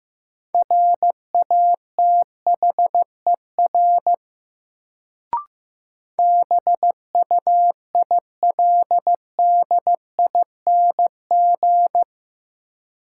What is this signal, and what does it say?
Morse code, heard as a keyed tone, about 15 wpm, 700 Hz. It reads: RATHER BUILDING
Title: Sentences from Top 700 Words - New Word Drill - 15wpm